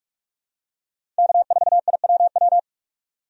Morse code, heard as a keyed tone, about 46 wpm, 700 Hz. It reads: K4IWW
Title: USA Call Signs - 5 Characters - 45wpm